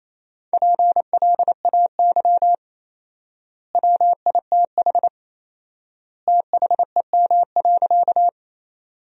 Morse code, heard as a keyed tone, about 28 wpm, 700 Hz. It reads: PLAY WITH THEM.
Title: 4x Word-Spacing - Sentences from Top 200 Words - 28wpm